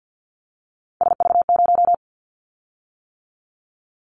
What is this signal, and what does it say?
Morse code, heard as a keyed tone, about 50 wpm, 700 Hz. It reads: SV0
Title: Callsign Prefixes for Top 50 DXCC - 50wpm